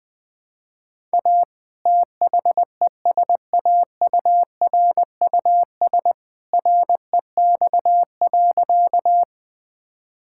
Morse code, heard as a keyed tone, about 20 wpm, 700 Hz. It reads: A THESAURUS REX.
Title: All-Age Appropriate Jokes - 20wpm